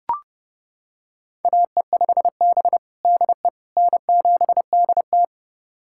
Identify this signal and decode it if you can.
Morse code, heard as a keyed tone, about 32 wpm, 700 Hz. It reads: AE5B DE N7DT
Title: USA Callsign DE Callsign - 4 Characters - 30wpm